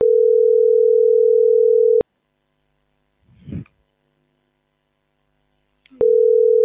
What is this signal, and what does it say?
Ringback tone (North America), 440 + 480 Hz.